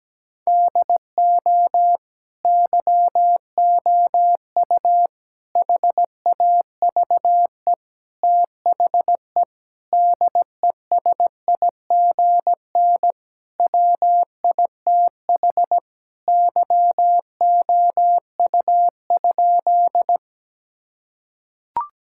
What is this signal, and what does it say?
Morse code, heard as a keyed tone, about 17 wpm, 700 Hz. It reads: DO YOU HAVE THE DESIGN WITH YOU?
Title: Sentences from Top 1000 Words - Review Part 5 - 17wpm